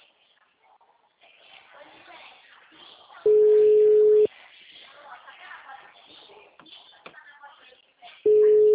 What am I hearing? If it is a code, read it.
Ringback tone (Europe, 425 Hz), with speech.
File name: out- -2007- - - .4